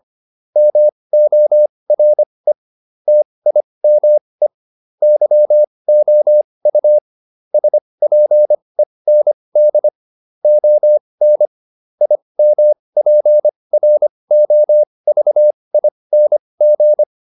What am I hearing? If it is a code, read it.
Morse code, heard as a keyed tone, about 25 wpm, spaced as for 20 wpm, 600 Hz. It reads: MORE TIME YOU SPEND ON IMPROVING